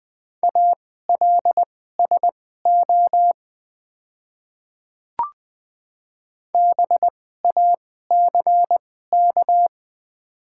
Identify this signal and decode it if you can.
Morse code, heard as a keyed tone, about 20 wpm, spaced as for 14 wpm, 700 Hz. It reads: ALSO BACK
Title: 2x Character Spacing - Top 200 Words - 20wpm